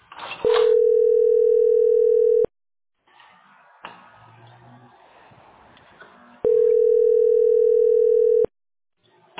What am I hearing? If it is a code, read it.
Ringback tone (North America), 440 + 480 Hz.